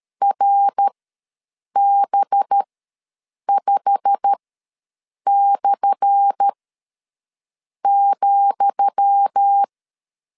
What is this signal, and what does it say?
Morse code, heard as a keyed tone, about 13 wpm, spaced as for 6 wpm, 800 Hz. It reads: RB5/,
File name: LEZIONI / 10 SIGNS / 10D COMA ws 12wpm 3618